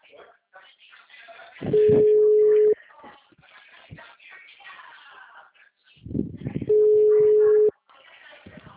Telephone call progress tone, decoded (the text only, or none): ringback tone